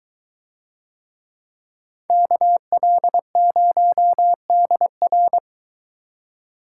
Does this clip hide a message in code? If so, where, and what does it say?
Yes: Morse code from 2.1 to 5.4 s, heard as a keyed tone, about 22 wpm, 700 Hz. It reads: KL0DR